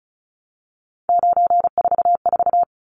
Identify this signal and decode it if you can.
Morse code, heard as a keyed tone, about 34 wpm, 700 Hz. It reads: 944
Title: Numbers - 3-Digit 35wpm